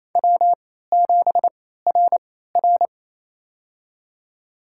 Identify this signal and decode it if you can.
Morse code, heard as a keyed tone, about 28 wpm, spaced as for 16 wpm, 700 Hz. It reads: W7RR
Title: USA Call Signs - 4 Characters - 3x Character Spacing - 28wpm